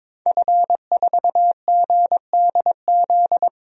Morse code, heard as a keyed tone, about 21 wpm, 700 Hz. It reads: F4GDZ
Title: DXCC-100 Call Signs - 5 Characters - 22wpm